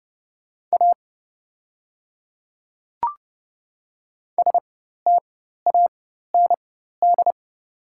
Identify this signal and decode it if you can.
Morse code, heard as a keyed tone, about 30 wpm, spaced as for 14 wpm, 700 Hz. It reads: A STAND
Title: 4x Character Spacing - Top 300 Words - 30wpm